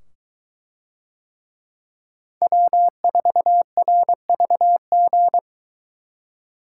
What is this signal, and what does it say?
Morse code, heard as a keyed tone, about 22 wpm, 700 Hz. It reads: W4RVG